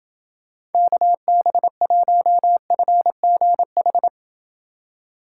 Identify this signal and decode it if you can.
Morse code, heard as a keyed tone, about 29 wpm, 700 Hz. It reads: KB1FGH